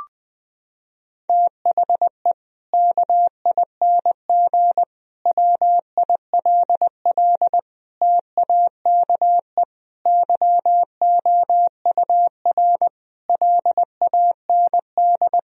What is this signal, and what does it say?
Morse code, heard as a keyed tone, about 20 wpm, 700 Hz. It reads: THE KING WILL TAKE YOUR LAND
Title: Sentences from Top 1000 Words - Review Part 4 - 20wpm